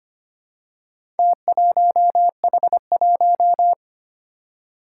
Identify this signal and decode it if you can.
Morse code, heard as a keyed tone, about 25 wpm, 700 Hz. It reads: T1H1
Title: Four Challenging Letter-Number Combinations 25wpm